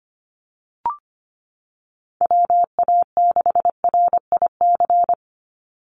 Morse code, heard as a keyed tone, about 25 wpm, 700 Hz. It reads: WA6RIC
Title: NA SOTA Chaser Callsigns 25wpm